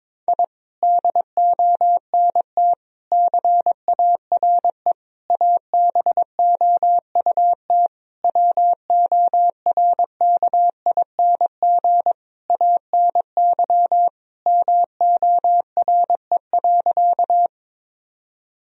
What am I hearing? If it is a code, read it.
Morse code, heard as a keyed tone, about 22 wpm, 700 Hz. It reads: I DONT CARE ABOUT WORKING ANY MORE.
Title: Sentences from Top 1000 Words - Review Part 4 - 22wpm